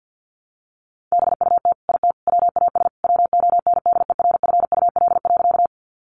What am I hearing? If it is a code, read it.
Morse code, heard as a keyed tone, about 50 wpm, 700 Hz. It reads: BUT IT WAS WONDERFUL.
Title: Sentences from Top 1100 Words 50wpm